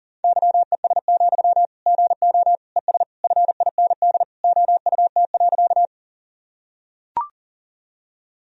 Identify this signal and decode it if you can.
Morse code, heard as a keyed tone, about 40 wpm, 700 Hz. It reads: YES, GO ES FIND OUT.
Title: Sentences from Top 500 Words and Abbreviations 40wpm